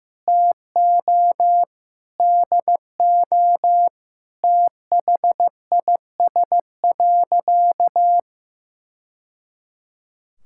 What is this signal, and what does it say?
Morse code, heard as a keyed tone, about 15 wpm, 700 Hz. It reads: TO DO THIS.
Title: Sentences from Top 1000 Words - Review Part 5 - 15wpm